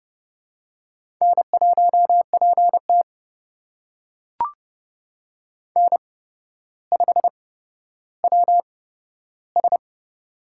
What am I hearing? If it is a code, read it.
Morse code, heard as a keyed tone, about 31 wpm, spaced as for 13 wpm, 700 Hz. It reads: N1PT N5WS
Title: USA Call Signs - 4 Characters - 8x Character Spacing - 30wpm